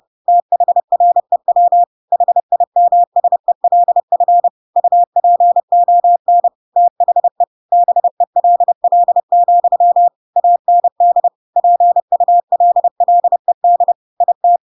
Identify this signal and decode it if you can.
Morse code, heard as a keyed tone, about 30 wpm, 700 Hz. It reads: THREW HIMSELF UPON THE BELL, AND PULLED IT
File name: Pinocchio / Pinocchio ch6 30 30 ik1hge